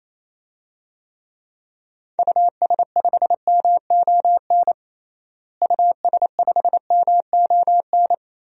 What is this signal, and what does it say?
Morse code, heard as a keyed tone, about 28 wpm, 700 Hz. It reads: US5MON US5MON
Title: DXCC-100 Call Signs - 6 Characters - Repeated 3x - 28wpm